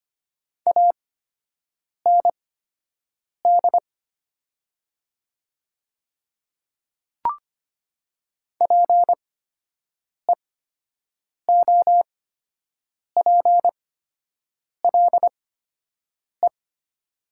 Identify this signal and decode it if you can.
Morse code, heard as a keyed tone, about 25 wpm, spaced as for 7 wpm, 700 Hz. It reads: AND PEOPLE
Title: 8x Character Spacing - Top 200 Words - 25wpm